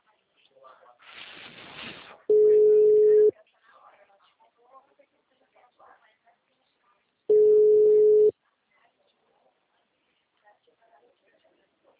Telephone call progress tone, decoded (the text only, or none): ringback tone